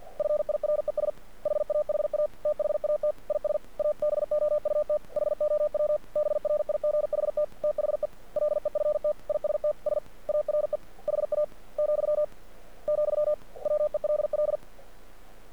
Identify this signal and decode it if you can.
Morse code, heard as a keyed tone, about 48 wpm, 600 Hz. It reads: FIRES HAHA THAT IS ABOUT HOW BRIGHT THE LEFTISTS ARE HA , , WELL
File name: archived / CRSnet